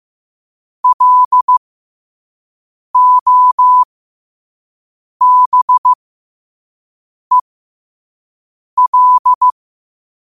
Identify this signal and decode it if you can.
Morse code, heard as a keyed tone, about 15 wpm, spaced as for 5 wpm, 1000 Hz. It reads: LOBEL
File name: qotdfiles / 5wpm